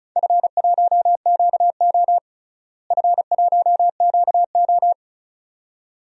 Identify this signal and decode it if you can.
Morse code, heard as a keyed tone, about 36 wpm, 700 Hz. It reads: F1QO F1QO
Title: DXCC-100 Call Signs - 4 Characters - Repeated 2x - 35wpm